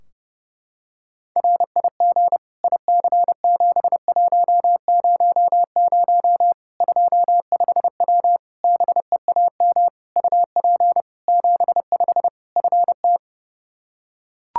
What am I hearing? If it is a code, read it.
Morse code, heard as a keyed tone, about 30 wpm, 700 Hz. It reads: RIG IC7100 25W BEAM UP 75 FT
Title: Practice QSO - Exchange 2 - Info - 30wpm